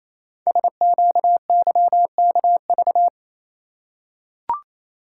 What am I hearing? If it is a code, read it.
Morse code, heard as a keyed tone, about 27 wpm, 700 Hz. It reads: SQYKV
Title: Code Groups - 5 Letters - 28wpm